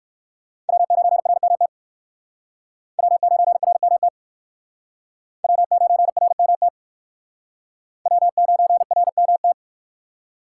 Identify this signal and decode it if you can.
Morse code, heard as a keyed tone, about 63 wpm, 700 Hz. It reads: W9RMT W9RMT W9RMT W9RMT